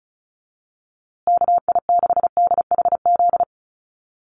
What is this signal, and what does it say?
Morse code, heard as a keyed tone, about 34 wpm, 700 Hz. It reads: KI6DHZ
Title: USA Call Signs Repeated 2x 35wpm